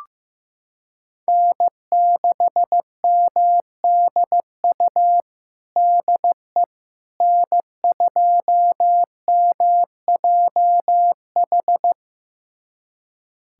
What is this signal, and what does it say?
Morse code, heard as a keyed tone, about 15 wpm, 700 Hz. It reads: N6MDU DE N2MJH
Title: USA Callsign DE Callsign - 5 Characters - 15wpm